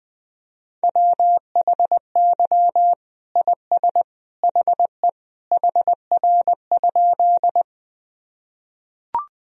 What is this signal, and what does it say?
Morse code, heard as a keyed tone, about 20 wpm, 700 Hz. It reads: WHY IS HE HR?
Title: Sentences from Top 200 Words and Abbreviations 20wpm